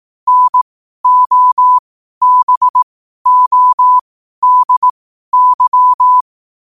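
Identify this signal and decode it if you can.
Morse code, heard as a keyed tone, about 18 wpm, spaced as for 12 wpm, 1000 Hz. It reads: NOBODY
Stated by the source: Morse code quote-of-the-day practice at 12 words per minute.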